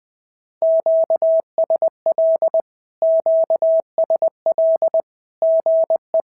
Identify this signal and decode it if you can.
Morse code, heard as a keyed tone, about 20 wpm, 690 Hz. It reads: QSL QSL GE
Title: SOTA QSO Practice 20wpm